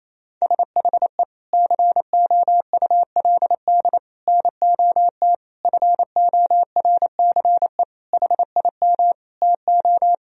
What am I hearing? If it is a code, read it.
Morse code, heard as a keyed tone, about 28 wpm, 700 Hz. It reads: SHE COULD NOT FORCE HIM TO
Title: Sentences from Top 1000 Words - Review Part 3 - 28wpm